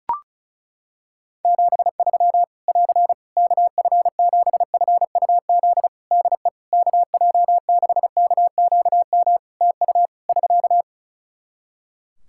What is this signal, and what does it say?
Morse code, heard as a keyed tone, about 34 wpm, 700 Hz. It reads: 73 <AR> KF7FUZ DE KJ6KQM TU <SK>
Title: Practice QSO - Exchange 4 - Ending - 35wpm